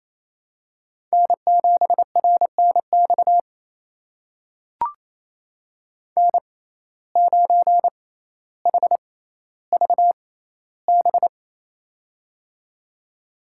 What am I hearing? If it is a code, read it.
Morse code, heard as a keyed tone, about 28 wpm, spaced as for 14 wpm, 700 Hz. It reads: N7RNX N9HVB
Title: USA Call Signs - 5 Characters - 6x Character Spacing - 28wpm